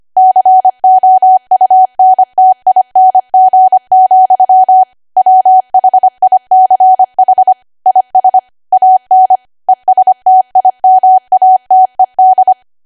Morse code, heard as a keyed tone, about 25 wpm, 750 Hz. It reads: COUNTING, WHICH IS AN ESTIMATED
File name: Archive / 25 WPM / 25WPM